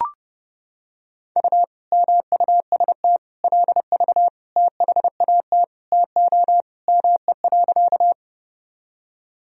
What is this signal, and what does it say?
Morse code, heard as a keyed tone, about 32 wpm, 700 Hz. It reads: U MUST LV THAT TO ME.
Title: Sentences from Top 400 Words and Abbreviations 30wpm